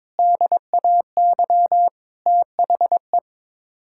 Morse code, heard as a keyed tone, about 22 wpm, 700 Hz. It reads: DAY THE
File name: practice / 22wpm / Sets of 8 Words - 22wpm